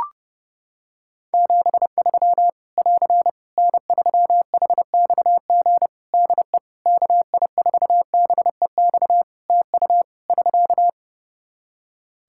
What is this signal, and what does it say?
Morse code, heard as a keyed tone, about 30 wpm, 700 Hz. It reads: 73 <AR> N3HXG DE KI4BEX TU <SK>
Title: K1USN SST QSO Practice 30wpm